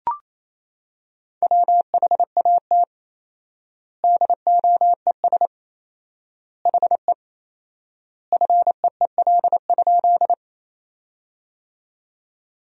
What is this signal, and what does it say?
Morse code, heard as a keyed tone, about 28 wpm, 700 Hz. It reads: WHAT DOES HE FEEL?
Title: 4x Word-Spacing - Sentences from Top 400 Words - 28wpm